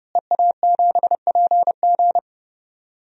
Morse code, heard as a keyed tone, about 31 wpm, 700 Hz. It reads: EA7PG
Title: DXCC-100 Call Signs - 5 Characters - Repeated 3x - 30wpm